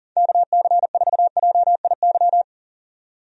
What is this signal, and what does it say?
Morse code, heard as a keyed tone, about 39 wpm, 700 Hz. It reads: KC4JIY
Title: USA Call Signs - 6 Characters - 40wpm